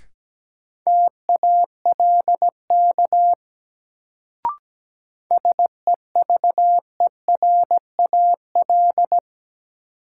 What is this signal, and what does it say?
Morse code, heard as a keyed tone, about 17 wpm, 700 Hz. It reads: TALK SEVERAL
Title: Sentences from Top 400 Words - New Word Drill - 17wpm